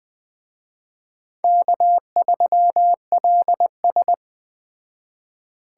Morse code, heard as a keyed tone, about 20 wpm, 700 Hz. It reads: K3LS